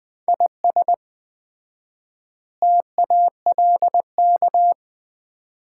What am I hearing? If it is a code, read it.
Morse code, heard as a keyed tone, about 20 wpm, 700 Hz. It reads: IS TALK
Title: 4x Word-Spacing - Sets of 5 Words - 20wpm